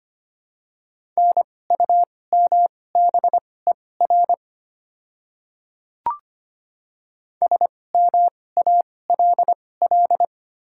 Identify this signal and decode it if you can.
Morse code, heard as a keyed tone, about 25 wpm, spaced as for 18 wpm, 700 Hz. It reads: NUMBER SMALL
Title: 2x Character Spacing - Top 200 Words - 25wpm